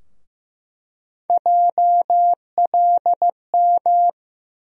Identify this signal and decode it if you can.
Morse code, heard as a keyed tone, about 15 wpm, 700 Hz. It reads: JLM
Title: USA Call Sign Postfixes - 3 Letters 15wpm